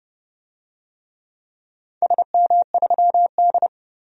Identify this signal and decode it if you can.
Morse code, heard as a keyed tone, about 30 wpm, 700 Hz. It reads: SM3D
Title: DXCC-100 Call Signs - 4 Characters - 30wpm